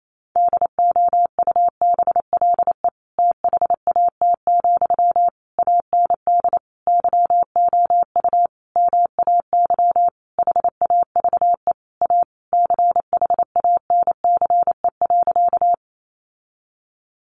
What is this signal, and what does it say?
Morse code, heard as a keyed tone, about 28 wpm, 700 Hz. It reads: DOUBLE THAT, AND YOU MAY HAVE A CHANCE.
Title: Sentences from Top 1400 Words 28wpm